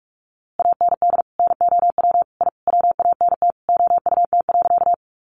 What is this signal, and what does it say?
Morse code, heard as a keyed tone, about 44 wpm, 700 Hz. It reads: AND NOW I WANT OUT.
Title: A Sound Of Thunder 45wpm